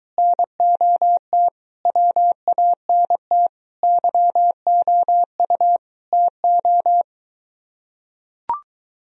Morse code, heard as a keyed tone, about 23 wpm, 700 Hz. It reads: NOT WANT YOU TO